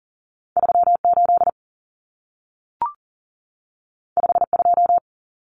Morse code, heard as a keyed tone, about 40 wpm, 700 Hz. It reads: 38 52
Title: Numbers - 2-Digit 40wpm